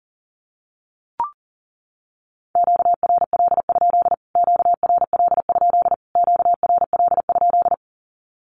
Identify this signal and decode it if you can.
Morse code, heard as a keyed tone, about 40 wpm, 700 Hz. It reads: QRL? QRL? QRL?
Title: Q-Codes Repeated 3x 40wpm